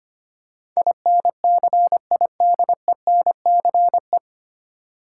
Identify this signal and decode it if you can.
Morse code, heard as a keyed tone, about 25 wpm, 700 Hz. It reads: INCIDENCE
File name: practice / 25wpm / 9 Letter Words - First 3 Letters Given - 25wpm